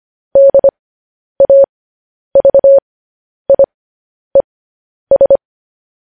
Morse code, heard as a keyed tone, about 25 wpm, spaced as for 10 wpm, 550 Hz. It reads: DAVIES
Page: news headlines in morse code at 10 wpm